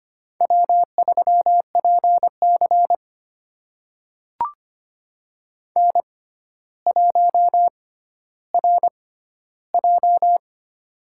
Morse code, heard as a keyed tone, about 25 wpm, spaced as for 13 wpm, 700 Hz. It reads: W3PC N1RJ
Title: USA Call Signs - 4 Characters - 6x Character Spacing - 25wpm